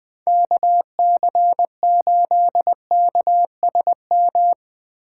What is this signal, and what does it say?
Morse code, heard as a keyed tone, about 20 wpm, 700 Hz. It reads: KC8KSM